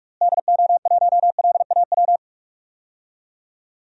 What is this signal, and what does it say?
Morse code, heard as a keyed tone, about 46 wpm, 700 Hz. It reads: DO1LAW
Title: DXCC-100 Call Signs - 6 Characters - Repeated 2x - 45wpm